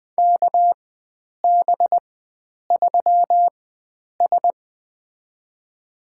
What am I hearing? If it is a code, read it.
Morse code, heard as a keyed tone, about 20 wpm, spaced as for 9 wpm, 700 Hz. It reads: KB3S
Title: USA Call Signs - 4 Characters - 4x Character Spacing - 20wpm